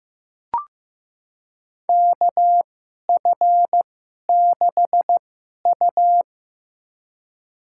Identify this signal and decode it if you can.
Morse code, heard as a keyed tone, about 15 wpm, spaced as for 11 wpm, 700 Hz. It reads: KF6U
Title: USA Call Signs - 4 Characters - 2x Character Spacing - 15wpm